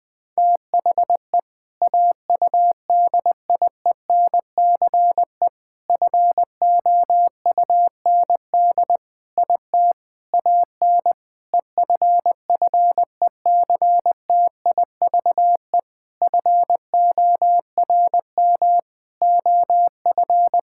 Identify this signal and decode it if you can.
Morse code, heard as a keyed tone, about 21 wpm, 700 Hz. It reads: THE AUDIENCE FOUND IT AN EFFECTIVE FORM OF
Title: Sentences from Top 1000 Words - Review Part 6 - 20wpm